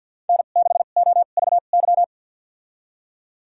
Morse code, heard as a keyed tone, about 52 wpm, spaced as for 37 wpm, 700 Hz. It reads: N6OVY
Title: USA Call Signs - 5 Characters - 2x Character Spacing - 50wpm